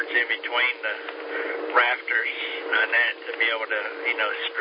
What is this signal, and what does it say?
Emergency Alert System (SAME) tones, heard as harsh buzzing bursts, with speech. It is heard from 0.5 to 3.9 s.